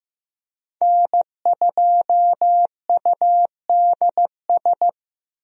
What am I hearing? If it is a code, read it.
Morse code, heard as a keyed tone, about 15 wpm, 700 Hz. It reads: N2UDS
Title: USA Call Signs - 5 Characters - Repeated 2x - 15wpm